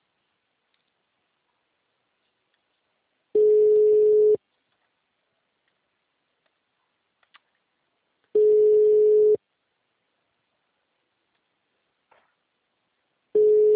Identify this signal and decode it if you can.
Ringback tone (Europe, 425 Hz).